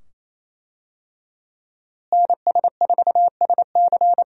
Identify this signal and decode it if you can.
Morse code, heard as a keyed tone, about 28 wpm, 700 Hz. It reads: NS4SC
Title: USA Call Signs - 5 Characters - Repeated 2x - 28wpm